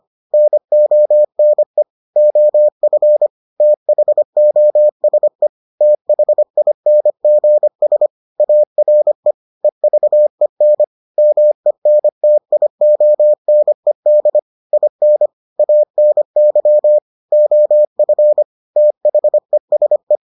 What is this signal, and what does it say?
Morse code, heard as a keyed tone, about 25 wpm, 600 Hz. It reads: NONE OF THOSE THINGS ARE EVEN MENTIONED IN ANY OF THESE